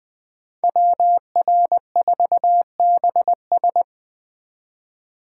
Morse code, heard as a keyed tone, about 20 wpm, 700 Hz. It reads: WR4BS
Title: USA Call Signs - 5 Characters - 3x Character Spacing - 20wpm